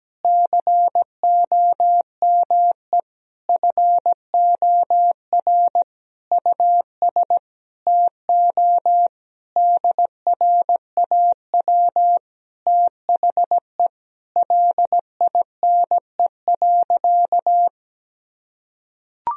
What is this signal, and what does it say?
Morse code, heard as a keyed tone, about 17 wpm, 700 Hz. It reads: COME FOR US TO DRAW THE LINE.
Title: Sentences from Top 1000 Words - Review Part 1 - 17wpm